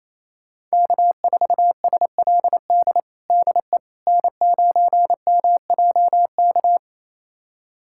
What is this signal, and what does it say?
Morse code, heard as a keyed tone, about 27 wpm, 700 Hz. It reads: K4SLD DE N9MJK
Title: USA Callsign DE Callsign - 5 Characters - 28wpm